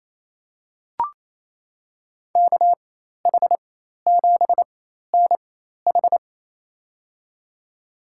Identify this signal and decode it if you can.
Morse code, heard as a keyed tone, about 28 wpm, spaced as for 13 wpm, 700 Hz. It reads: KH7NH
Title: USA Call Signs - 5 Characters - 4x Character Spacing - 28wpm